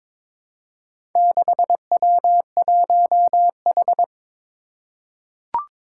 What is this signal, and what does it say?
Morse code, heard as a keyed tone, about 22 wpm, 700 Hz. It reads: 6W1H
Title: Four Challenging Letter-Number Combinations 22wpm